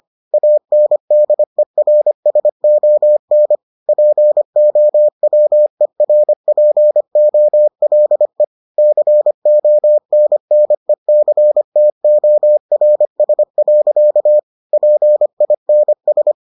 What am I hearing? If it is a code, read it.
Morse code, heard as a keyed tone, about 25 wpm, 600 Hz. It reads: ANDERSON POWERPOLE CONNECTORS. PINS